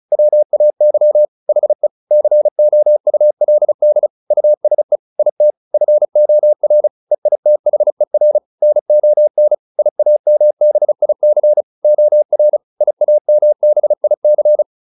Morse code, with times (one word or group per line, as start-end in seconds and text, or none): 0.1-1.2 WAY
1.5-1.9 HE
2.1-4.1 COULD
4.3-5.0 USE
5.2-5.5 IT
5.7-6.9 FOR
7.1-8.4 EITHER
8.6-9.5 NON
9.8-11.6 IAMBIC
11.8-12.6 OR
12.8-14.6 IAMBIC